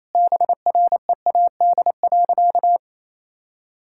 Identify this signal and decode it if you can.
Morse code, heard as a keyed tone, about 28 wpm, 700 Hz. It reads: BREAD.
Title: Sentences from Top 1600 Words 28wpm